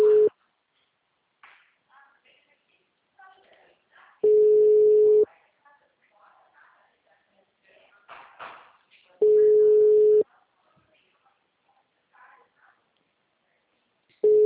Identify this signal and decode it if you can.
Ringback tone (Europe, 425 Hz), with speech.
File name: out- -2011- - - .44485